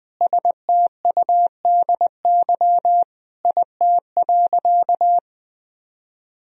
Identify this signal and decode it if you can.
Morse code, heard as a keyed tone, about 21 wpm, 700 Hz. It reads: STUDY IT.
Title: Sentences from Top 1000 Words - Review Part 1 - 20wpm